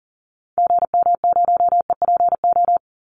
Morse code, heard as a keyed tone, about 40 wpm, 700 Hz. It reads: GM0EPO
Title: Call Signs - 40wpm